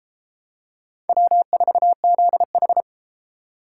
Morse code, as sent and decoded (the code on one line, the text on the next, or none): .-- ....- --.. ....
W4ZH